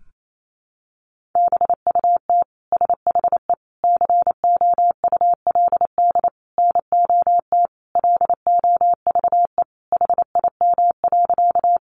Morse code, heard as a keyed tone, about 28 wpm, 700 Hz. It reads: BUT SHE COULD NOT LOVE HIM.
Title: Sentences from Top 600 Words 28wpm